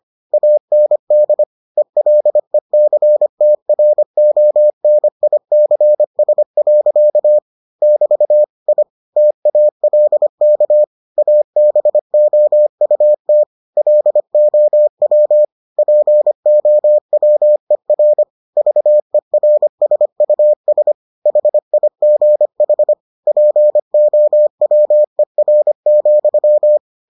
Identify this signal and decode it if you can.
Morse code, heard as a keyed tone, about 25 wpm, 600 Hz. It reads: AND ELECTRONICS. <BT> I TALK ABOUT LOW POWER VERSUS HIGH POWER,